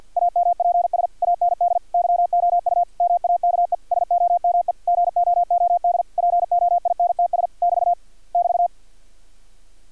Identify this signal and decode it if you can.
Morse code, heard as a keyed tone, about 48 wpm, 700 Hz. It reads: AMPS AND YOU MAKE SOME GOOD POINTS <BT> <BT>
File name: archived / CRSnet